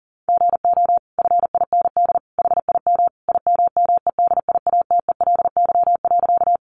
Morse code, heard as a keyed tone, about 40 wpm, 700 Hz. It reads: GO FIND HIM IMMEDIATELY.
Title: Sentences from Top 1100 Words 40wpm